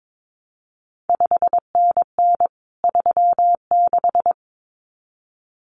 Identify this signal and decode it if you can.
Morse code, heard as a keyed tone, about 22 wpm, 700 Hz. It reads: 5NN 36
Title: CQ WW DX Contest Exchange Encore - 22wpm